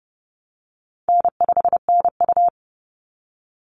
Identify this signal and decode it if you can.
Morse code, heard as a keyed tone, about 30 wpm, 700 Hz. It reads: N5NU
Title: Call Signs Repeated 3x - 30wpm